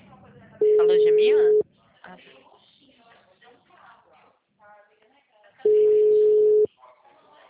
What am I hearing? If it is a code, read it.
Ringback tone (Europe, 425 Hz), with speech.